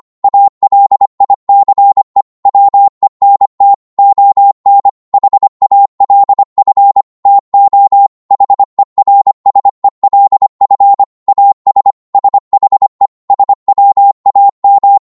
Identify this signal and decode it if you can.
Morse code, heard as a keyed tone, about 25 wpm, 800 Hz. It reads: ALICE WENT ON HALF TO HERSELF AS SHE SWAM